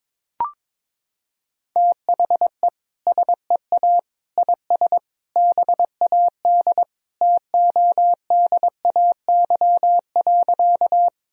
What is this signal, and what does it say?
Morse code, heard as a keyed tone, about 21 wpm, 700 Hz. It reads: THE SEA IS BAD TODAY.
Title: Sentences from Top 1000 Words - Review Part 1 - 22wpm